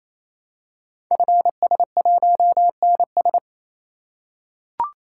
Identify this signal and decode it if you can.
Morse code, heard as a keyed tone, about 27 wpm, 700 Hz. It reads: FS1NS
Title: Code Groups - 5 Letter:Numbers - 28wpm